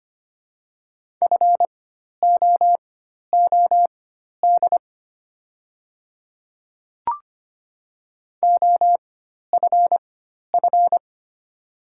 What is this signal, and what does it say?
Morse code, heard as a keyed tone, about 25 wpm, spaced as for 12 wpm, 700 Hz. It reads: FOOD OFF
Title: 4x Character Spacing - Top 500 Words - 25wpm